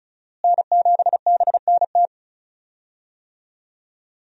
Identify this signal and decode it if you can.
Morse code, heard as a keyed tone, about 34 wpm, 700 Hz. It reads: N7BNT
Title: USA Call Signs - 5 Characters - Repeated 2x - 35wpm